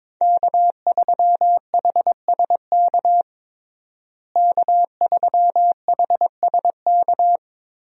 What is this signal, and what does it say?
Morse code, heard as a keyed tone, about 22 wpm, 700 Hz. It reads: K3HSK K3HSK
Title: USA Call Signs - 5 Characters - Repeated 2x - 22wpm